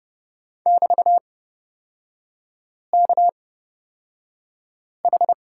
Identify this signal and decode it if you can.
Morse code, heard as a keyed tone, about 30 wpm, spaced as for 5 wpm, 700 Hz. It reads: <BT>KH
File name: Lesson 36e - BT - Three Characters 30wpm